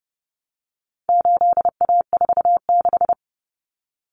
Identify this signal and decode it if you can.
Morse code, heard as a keyed tone, about 30 wpm, 700 Hz. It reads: 8A46
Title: Common Cut Numbers - 4-Digit 30wpm